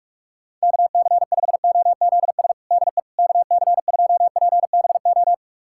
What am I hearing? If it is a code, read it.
Morse code, heard as a keyed tone, about 46 wpm, 700 Hz. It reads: KC5OZS DE KC2PBO
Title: USA Callsign DE Callsign 45wpm